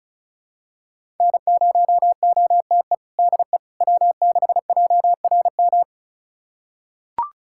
Morse code, heard as a keyed tone, about 34 wpm, 700 Hz. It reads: N0OTE DE W6JRM
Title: USA Callsign DE Callsign - 5 Characters - 35wpm